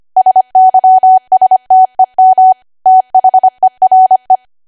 Morse code, heard as a keyed tone, about 25 wpm, 750 Hz. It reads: SYSTEM THERE
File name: cwa / W1AW-Level3 / 25WPM